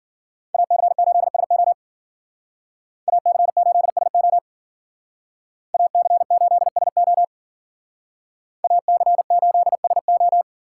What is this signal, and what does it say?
Morse code, heard as a keyed tone, about 53 wpm, 700 Hz. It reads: AC8SO AC8SO AC8SO AC8SO